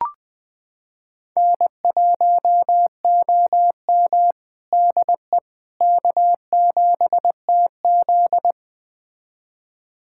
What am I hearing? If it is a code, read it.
Morse code, heard as a keyed tone, about 20 wpm, 700 Hz. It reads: N1OM DE K7TZ
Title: USA Callsign DE Callsign - 4 Characters - 20wpm